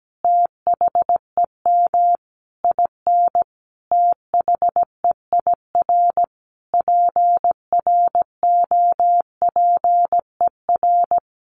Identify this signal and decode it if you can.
Morse code, heard as a keyed tone, about 17 wpm, 700 Hz. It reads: THEM IN THEIR PROPER
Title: Sentences from Top 800 Words 17wpm